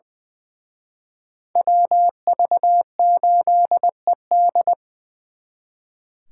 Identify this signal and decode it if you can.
Morse code, heard as a keyed tone, about 24 wpm, 700 Hz. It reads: WV8ED
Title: Speed Racing - USA 2x2 Call Signs 20wpm